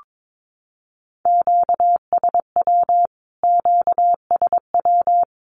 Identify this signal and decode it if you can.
Morse code, heard as a keyed tone, about 22 wpm, 700 Hz. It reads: QSW QSW
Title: Q-Codes Repeated 3x 22wpm